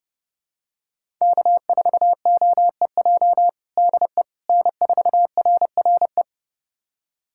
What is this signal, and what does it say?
Morse code, heard as a keyed tone, about 30 wpm, 700 Hz. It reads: K4OEJ DE N4RRE
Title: USA Callsign DE Callsign - 5 Characters - 30wpm